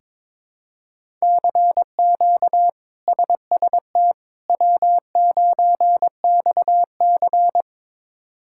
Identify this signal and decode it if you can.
Morse code, heard as a keyed tone, about 22 wpm, 700 Hz. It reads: CQ SST W9XC
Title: K1USN SST QSO Practice - 22wpm